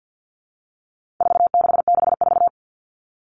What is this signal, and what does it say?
Morse code, heard as a keyed tone, about 50 wpm, 700 Hz. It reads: 4664
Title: Numbers - 4-Digit 50wpm